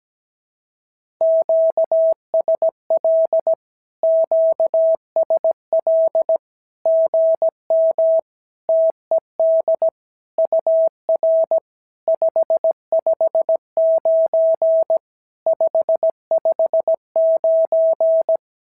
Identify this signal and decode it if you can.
Morse code, heard as a keyed tone, about 17 wpm, 650 Hz. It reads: QSL QSL GM TED UR 559 559
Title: SOTA QSO Practice 17wpm